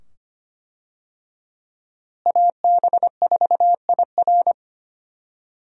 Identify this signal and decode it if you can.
Morse code, heard as a keyed tone, about 25 wpm, 700 Hz. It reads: AB4IR
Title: USA Call Signs - 5 Characters - Repeated 2x - 25wpm